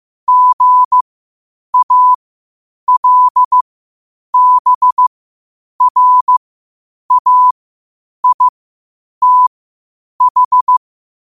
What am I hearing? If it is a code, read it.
Morse code, heard as a keyed tone, about 15 wpm, spaced as for 8 wpm, 1000 Hz. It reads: GALBRAITH
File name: qotdfiles / 8wpm